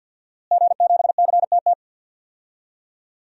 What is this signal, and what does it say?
Morse code, heard as a keyed tone, about 49 wpm, 700 Hz. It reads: G7CTT
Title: DXCC-100 Call Signs - 5 Characters - Repeated 3x - 50wpm